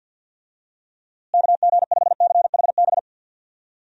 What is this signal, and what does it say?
Morse code, heard as a keyed tone, about 52 wpm, 700 Hz. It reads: KG5CHB
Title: USA Call Signs - 6 Characters - Repeated 2x - 50wpm